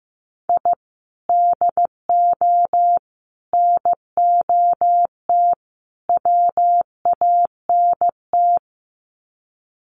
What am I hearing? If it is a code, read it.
Morse code, heard as a keyed tone, about 15 wpm, 700 Hz. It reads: I DO NOT WANT
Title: Sets of 4 Words 15wpm